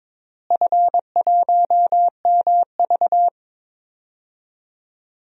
Morse code, heard as a keyed tone, about 22 wpm, 700 Hz. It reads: F1MV
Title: DXCC-100 Call Signs - 4 Characters - Repeated 2x - 22wpm